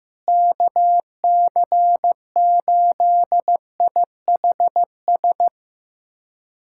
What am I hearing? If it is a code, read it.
Morse code, heard as a keyed tone, about 16 wpm, 700 Hz. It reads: KC8IHS
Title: USA Callsign DE Callsign 15wpm